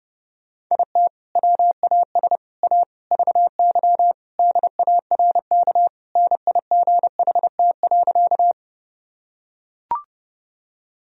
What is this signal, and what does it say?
Morse code, heard as a keyed tone, about 32 wpm, 700 Hz. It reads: IT WAS A VY DARK NIGHT.
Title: Sentences from Top 500 Words and Abbreviations 30wpm